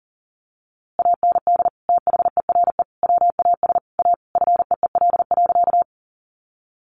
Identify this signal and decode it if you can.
Morse code, heard as a keyed tone, about 37 wpm, 700 Hz. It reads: AND THERE WAS A FEEL.
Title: A Sound Of Thunder 40wpm